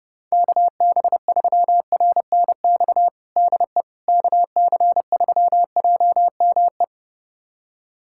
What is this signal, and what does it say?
Morse code, heard as a keyed tone, about 30 wpm, 700 Hz. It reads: KB3RNX DE KC3JME
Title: USA Callsign DE Callsign 30wpm